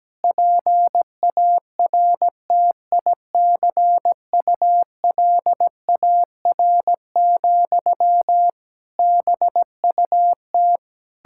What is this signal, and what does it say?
Morse code, heard as a keyed tone, about 17 wpm, 700 Hz. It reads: PARTICULAR, BUT
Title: Sentences from Top 1000 Words - Review Part 5 - 17wpm